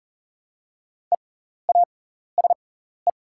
Morse code, heard as a keyed tone, about 40 wpm, spaced as for 14 wpm, 690 Hz. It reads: EASE
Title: 6x Character Spacing - Top 500 Words - 40wpm